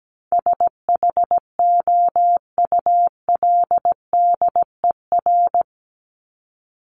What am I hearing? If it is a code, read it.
Morse code, heard as a keyed tone, about 17 wpm, 700 Hz. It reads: SHOULDER
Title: Top 1501-1750 Words 17wpm